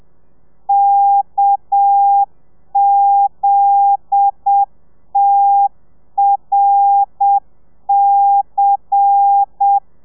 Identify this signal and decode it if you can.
Morse code, heard as a keyed tone, about 7 wpm, 800 Hz. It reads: KZTRC
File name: telegrafia / i2tfj / Lez10 30